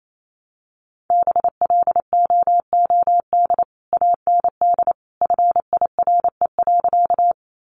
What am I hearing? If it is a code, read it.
Morse code, heard as a keyed tone, about 28 wpm, 700 Hz. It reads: BLOOD AND FIRE.
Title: Sentences from Top 1100 Words 28wpm